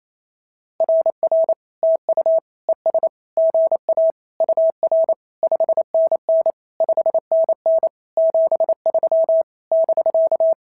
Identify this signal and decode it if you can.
Morse code, heard as a keyed tone, about 28 wpm, 650 Hz. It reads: RR TU ES GA UR 5NN 5NN 73 <BK>
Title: SOTA QSO Practice 28wpm